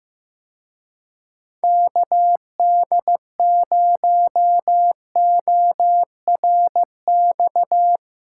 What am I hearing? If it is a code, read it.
Morse code, heard as a keyed tone, about 15 wpm, 700 Hz. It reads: KD0ORX
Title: Speed Racing - USA Call Signs 15wpm